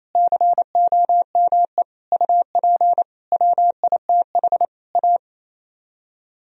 Morse code, heard as a keyed tone, about 37 wpm, 700 Hz. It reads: COME UP WITH A